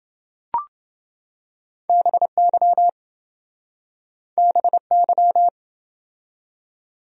Morse code, heard as a keyed tone, about 26 wpm, 700 Hz. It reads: BY BY